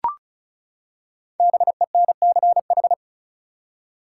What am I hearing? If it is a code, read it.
Morse code, heard as a keyed tone, about 36 wpm, 700 Hz. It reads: BENCH
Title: Five Letter Words - 35wpm